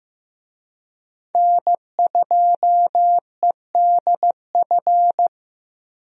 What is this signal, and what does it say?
Morse code, heard as a keyed tone, about 15 wpm, 700 Hz. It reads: N2EDF
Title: USA Call Signs - 5 Characters - Repeated 3x - 15wpm